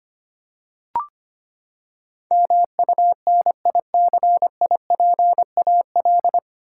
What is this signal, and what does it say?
Morse code, heard as a keyed tone, about 25 wpm, 700 Hz. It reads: MUNICIPAL
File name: Nine Letter Words - 25wpm